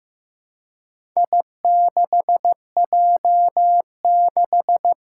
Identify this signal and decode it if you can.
Morse code, heard as a keyed tone, about 15 wpm, 700 Hz. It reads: I6J6
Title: Four Challenging Letter-Number Combinations 15wpm